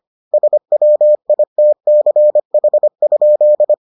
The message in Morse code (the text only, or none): SWITCH?